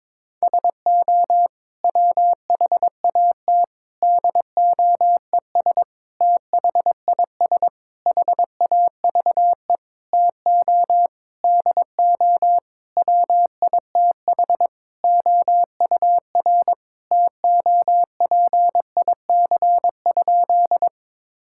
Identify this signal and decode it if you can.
Morse code, heard as a keyed tone, about 22 wpm, 700 Hz. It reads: SO WHAT DOES THIS HAVE TO DO WITH OUR TOPIC?
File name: practice / 22wpm / Sentences with 8-Word Phrases - 22wpm